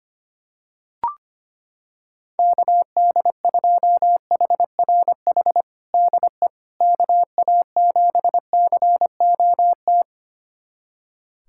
Morse code, heard as a keyed tone, about 25 wpm, 700 Hz. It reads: KD2HRH DE KA7COT
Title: USA Callsign DE Callsign 25wpm